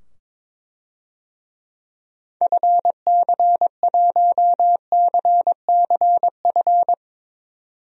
Morse code, heard as a keyed tone, about 22 wpm, 700 Hz. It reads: FC1CCF
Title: DXCC-100 Call Signs - 6 Characters - 22wpm